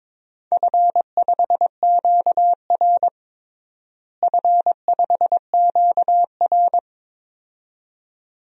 Morse code, heard as a keyed tone, about 22 wpm, 700 Hz. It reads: F5QR F5QR
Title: DXCC-100 Call Signs - 4 Characters - Repeated 2x - 22wpm